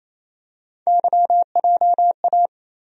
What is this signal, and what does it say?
Morse code, heard as a keyed tone, about 28 wpm, 700 Hz. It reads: YJA
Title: USA Call Sign Postfixes - 3 Letters 28wpm